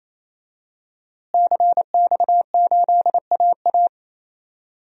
Morse code, heard as a keyed tone, about 28 wpm, 700 Hz. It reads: CX8AA
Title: DXCC-100 Call Signs - 5 Characters - Repeated 3x - 28wpm